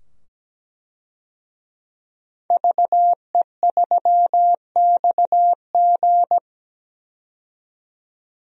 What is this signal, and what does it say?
Morse code, heard as a keyed tone, about 17 wpm, 700 Hz. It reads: VE3XG
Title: DXCC-100 Call Signs - 5 Characters - 17wpm